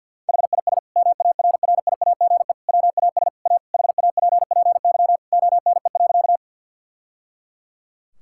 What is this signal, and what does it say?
Morse code, heard as a keyed tone, about 50 wpm, 700 Hz. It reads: HIS MARRIAGE WAS A HAPPY ONE.
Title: Sentences from Top 1000 Words - Review Part 5 - 50wpm